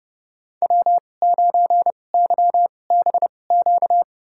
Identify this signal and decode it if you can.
Morse code, heard as a keyed tone, about 31 wpm, spaced as for 22 wpm, 700 Hz. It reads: W9YBQ
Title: USA Call Signs - 5 Characters - 2x Character Spacing - 30wpm